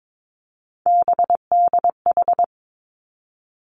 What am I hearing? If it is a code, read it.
Morse code, heard as a keyed tone, about 22 wpm, 700 Hz. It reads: BDH
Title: Three Challenging Letter-Number Combinations 22wpm